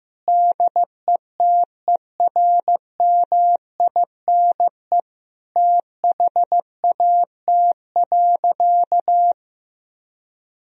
Morse code, heard as a keyed tone, about 15 wpm, 700 Hz. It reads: DETERMINE THAT.
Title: Sentences from Top 1000 Words - Review Part 4 - 15wpm